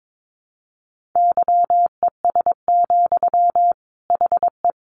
Morse code, heard as a keyed tone, about 22 wpm, 700 Hz. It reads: YES, HE
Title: Sentences from Top 500 Words 22wpm